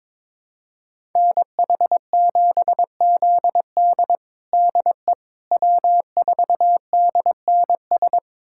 Morse code, heard as a keyed tone, about 22 wpm, 700 Hz. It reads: NH7ZD DE W4DNS
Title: USA Callsign DE Callsign - 5 Characters - 22wpm